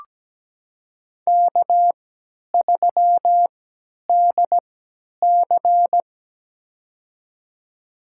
Morse code, heard as a keyed tone, about 17 wpm, spaced as for 10 wpm, 700 Hz. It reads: K3DC
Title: USA Call Signs - 4 Characters - 3x Character Spacing - 17wpm